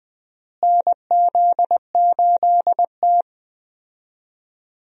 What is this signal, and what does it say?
Morse code, heard as a keyed tone, about 20 wpm, 700 Hz. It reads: NZ8T
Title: USA Call Signs - 4 Characters - Repeated 3x - 20wpm